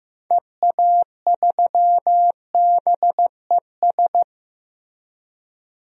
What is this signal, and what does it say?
Morse code, heard as a keyed tone, about 15 wpm, 700 Hz. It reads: EA3BES
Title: DXCC-100 Call Signs - 6 Characters - Repeated 3x - 15wpm